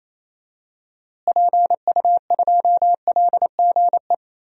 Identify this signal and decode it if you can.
Morse code, heard as a keyed tone, about 28 wpm, 700 Hz. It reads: PU2LGE
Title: DXCC-100 Call Signs - 6 Characters - Repeated 2x - 28wpm